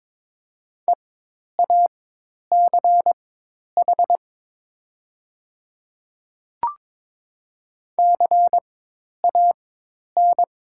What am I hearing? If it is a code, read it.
Morse code, heard as a keyed tone, about 22 wpm, spaced as for 10 wpm, 700 Hz. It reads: EACH CAN
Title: 4x Character Spacing - Top 100 Words - 22wpm